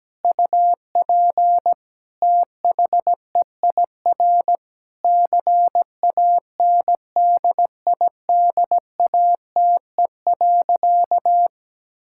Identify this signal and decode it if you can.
Morse code, heard as a keyed tone, about 17 wpm, 700 Hz. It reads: UP THEIR CANDIDATE.
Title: Sentences from Top 1600 Words 17wpm